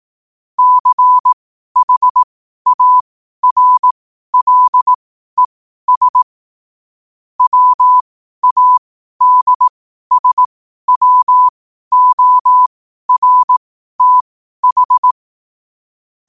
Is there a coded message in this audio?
Yes: Morse code from 0.6 to 15.1 s, heard as a keyed tone, about 18 wpm, spaced as for 12 wpm, 1000 Hz. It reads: CHARLES WADSWORTH